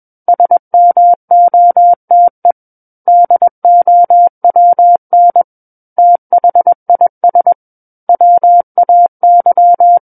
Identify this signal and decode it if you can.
Morse code, heard as a keyed tone, about 21 wpm, 700 Hz. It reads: SMOTE DOWN THIS WAY
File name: Book One / 12-Chapter 12 - 21 WPM